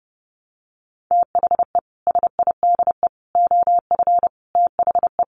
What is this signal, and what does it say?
Morse code, heard as a keyed tone, about 28 wpm, 700 Hz. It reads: THE SIDE OF THE
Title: Sets of 4 Words 30wpm